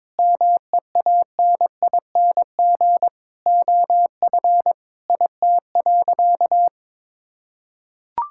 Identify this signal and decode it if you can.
Morse code, heard as a keyed tone, about 21 wpm, 700 Hz. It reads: MEANING OF IT.
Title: Sentences from Top 1000 Words - Review Part 6 - 22wpm